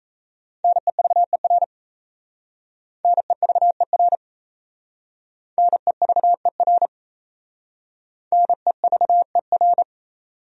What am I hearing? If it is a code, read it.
Morse code, heard as a keyed tone, about 36 wpm, 700 Hz. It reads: NEVER NEVER NEVER NEVER